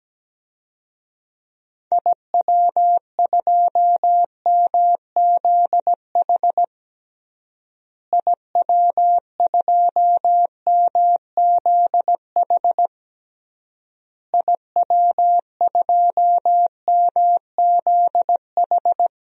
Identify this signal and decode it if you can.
Morse code, heard as a keyed tone, about 17 wpm, 700 Hz. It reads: IW2MZH IW2MZH IW2MZH
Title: DXCC-100 Call Signs - 6 Characters - Repeated 3x - 17wpm